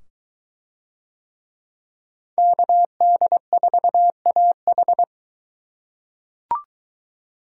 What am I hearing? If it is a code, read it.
Morse code, heard as a keyed tone, about 21 wpm, 700 Hz. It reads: KD4AH